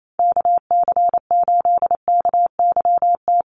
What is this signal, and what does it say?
Morse code, heard as a keyed tone, about 28 wpm, 700 Hz. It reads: KC8KYT
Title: USA Call Signs 28wpm